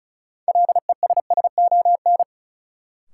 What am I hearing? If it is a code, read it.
Morse code, heard as a keyed tone, about 34 wpm, 700 Hz. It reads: LESSON
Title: Sentences from Top 1200 Words - New Word Drill - 35wpm